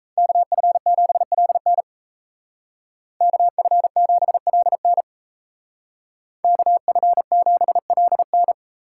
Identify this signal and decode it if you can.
Morse code, heard as a keyed tone, about 33 wpm, 700 Hz. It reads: KF7LN KF7LN KF7LN
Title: Speed Racing - USA 2x2 Call Signs 28wpm